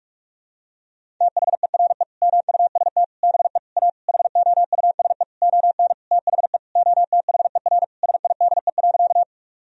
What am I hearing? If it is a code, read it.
Morse code, heard as a keyed tone, about 41 wpm, 700 Hz. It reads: THERE MUST BE A HOUSE ON THE OTHER SIDE.
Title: Sentences from Top 1000 Words - Review Part 1 - 45wpm